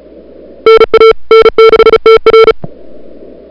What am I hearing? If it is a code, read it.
Morse code, heard as a keyed tone, about 34 wpm, 440 Hz. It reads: NA N6TR
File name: CWSprint / February 2007 CW Sprint / 01-2007 CW Sprint Hour1